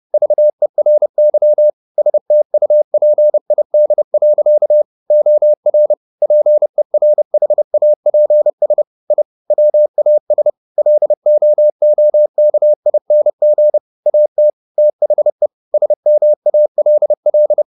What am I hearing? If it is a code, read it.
Morse code, heard as a keyed tone, about 30 wpm, 600 Hz. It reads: VERY STUPID. OR PERHAPS I WAS LOOKING AT THE SMALL
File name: the old man and the sea / the old man and the sea 1 30wpm